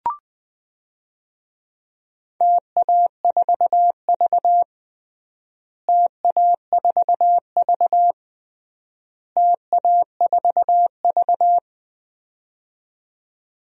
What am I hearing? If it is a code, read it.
Morse code, heard as a keyed tone, about 21 wpm, 700 Hz. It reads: TA4V TA4V TA4V
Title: DXCC-100 Call Signs - 4 Characters - Repeated 3x - 20wpm